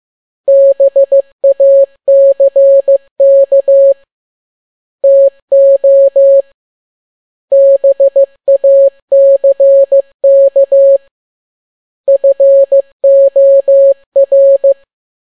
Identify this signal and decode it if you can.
Morse code, heard as a keyed tone, about 15 wpm, 550 Hz. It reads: BACK TO BACK FOR
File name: n6tt p1 15C13